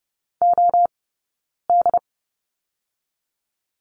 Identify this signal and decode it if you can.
Morse code, heard as a keyed tone, about 30 wpm, 700 Hz. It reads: O D
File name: course / 30:10wpm / 086 Lesson 15d - Number 5 - Two Characters 30wpm